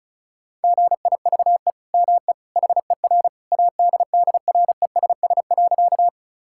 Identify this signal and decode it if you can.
Morse code, heard as a keyed tone, about 34 wpm, 700 Hz. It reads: GIVE ME HER ADDRESS.
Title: Sentences from Top 1000 Words - Review Part 6 - 35wpm